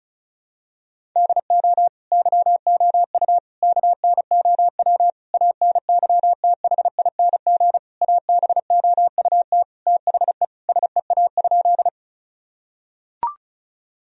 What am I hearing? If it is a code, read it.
Morse code, heard as a keyed tone, about 34 wpm, 700 Hz. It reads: DO YOU KNOW ANYTHING ABOUT THE SEA?
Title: Sentences from Top 1000 Words - Review Part 1 - 35wpm